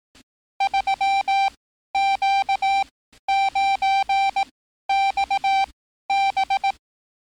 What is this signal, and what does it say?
Morse code, heard as a keyed tone, about 18 wpm, spaced as for 12 wpm, 780 Hz. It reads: 3Q9XB
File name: training / NAVPERS 11049to11060 / NAVPERS 11054 RA RB